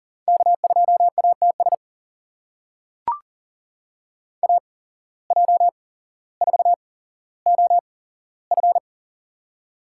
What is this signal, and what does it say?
Morse code, heard as a keyed tone, about 40 wpm, spaced as for 17 wpm, 700 Hz. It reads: K2ATS AJ4OF
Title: USA Call Signs - 5 Characters - 8x Character Spacing - 40wpm